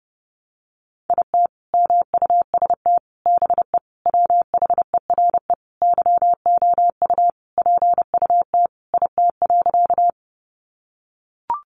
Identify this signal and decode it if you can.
Morse code, heard as a keyed tone, about 30 wpm, 700 Hz. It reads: IT MUST BE WHERE YOU PUT IT.
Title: Sentences from Top 200 Words 30wpm